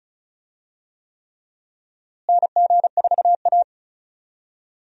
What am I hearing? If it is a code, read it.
Morse code, heard as a keyed tone, about 36 wpm, 700 Hz. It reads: NG4A
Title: USA Call Signs - 4 Characters - Repeated 3x - 35wpm